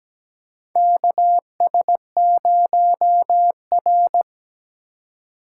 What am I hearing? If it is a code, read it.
Morse code, heard as a keyed tone, about 17 wpm, 700 Hz. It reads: KS0R
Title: USA Call Signs - 4 Characters - 4x Character Spacing - 17wpm